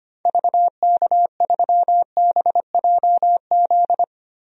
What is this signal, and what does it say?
Morse code, heard as a keyed tone, about 25 wpm, 700 Hz. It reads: VK3BJZ
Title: DXCC-100 Call Signs - 6 Characters - Repeated 3x - 25wpm